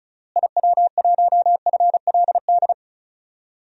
Morse code, heard as a keyed tone, about 36 wpm, 700 Hz. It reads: IW1FLD
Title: DXCC-100 Call Signs - 6 Characters - Repeated 3x - 35wpm